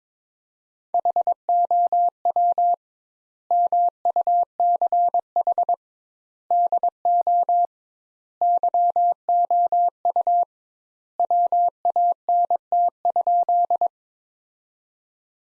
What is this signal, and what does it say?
Morse code, heard as a keyed tone, about 22 wpm, 700 Hz. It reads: HOW MUCH DO YOU WANT?
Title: Sentences Top 200 Words - 22wpm